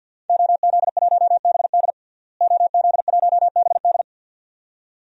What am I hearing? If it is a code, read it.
Morse code, heard as a keyed tone, about 48 wpm, 700 Hz. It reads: OZ1BD OZ1BD
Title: DXCC-100 Call Signs - 5 Characters - Repeated 2x - 50wpm